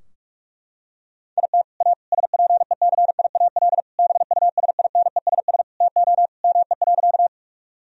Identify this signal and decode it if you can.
Morse code, heard as a keyed tone, about 41 wpm, 700 Hz. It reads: IT A SPECIAL BUSINESS TO ME.
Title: Sentences from Top 1000 Words - Review Part 3 - 45wpm